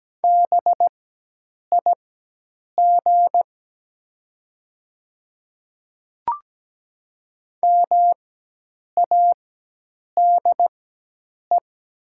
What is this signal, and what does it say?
Morse code, heard as a keyed tone, about 17 wpm, spaced as for 8 wpm, 700 Hz. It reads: BIG MADE
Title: 4x Character Spacing - Top 300 Words - 17wpm